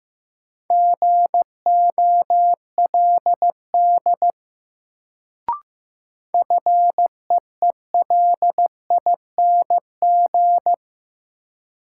Morse code, heard as a keyed tone, about 15 wpm, 700 Hz. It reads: GOLD FEELING
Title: Sentences from Top 1000 Words - Review Part 3 - New Word Drill - 15wpm